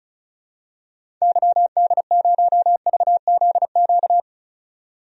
Morse code, heard as a keyed tone, about 36 wpm, 700 Hz. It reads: YD0VZQ
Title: DXCC-100 Call Signs - 6 Characters - 35wpm